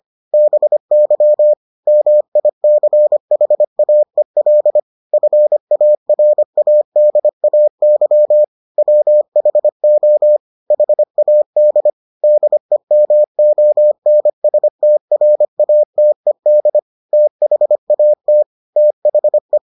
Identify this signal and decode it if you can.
Morse code, heard as a keyed tone, about 25 wpm, 600 Hz. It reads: BY MICHAEL FARADAY WHO HAD DEMONSTRATED THAT THE